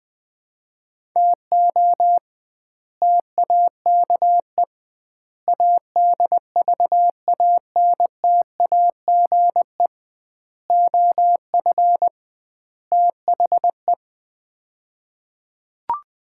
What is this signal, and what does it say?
Morse code, heard as a keyed tone, about 20 wpm, 700 Hz. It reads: TO TAKE ADVANTAGE OF THE
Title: 2x Word-Spacing - Sets of 5 Words - 20wpm